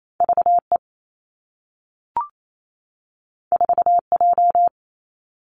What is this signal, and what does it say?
Morse code, heard as a keyed tone, about 28 wpm, 700 Hz. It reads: V E 4 J
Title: Two Challenging Letter-Number Combinations 28wpm